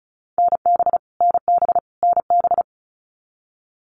Morse code, heard as a keyed tone, about 34 wpm, 700 Hz. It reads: NB NB NB
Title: Canadian Provinces and Territories - Repeated 3x - 35wpm